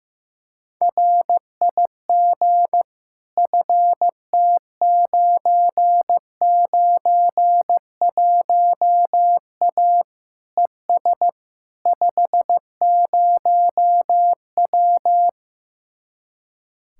Morse code, heard as a keyed tone, about 15 wpm, 700 Hz. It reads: RIG FT991A ES 50W
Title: Speed-Racing Common QSO Phrases Encore - 15wpm